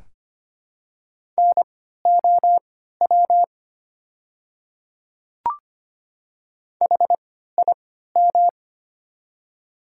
Morse code, heard as a keyed tone, about 25 wpm, spaced as for 14 wpm, 700 Hz. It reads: NOW HIM
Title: 3x Character Spacing - Top 100 Words - 25wpm